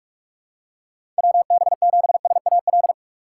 Morse code, heard as a keyed tone, about 46 wpm, 700 Hz. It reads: WB7SAL
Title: USA Call Signs - 6x Character Spacing - 45wpm